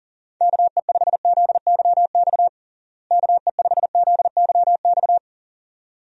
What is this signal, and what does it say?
Morse code, heard as a keyed tone, about 40 wpm, 700 Hz. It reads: KE5ZYX KE5ZYX
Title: USA Call Signs - 6 Characters - Repeated 2x - 40wpm